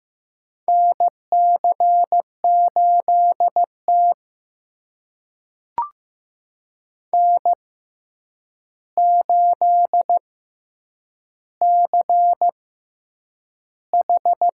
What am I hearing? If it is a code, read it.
Morse code, heard as a keyed tone, about 15 wpm, spaced as for 8 wpm, 700 Hz. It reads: NC8T N8CH
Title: USA Call Signs - 4 Characters - 6x Character Spacing - 15wpm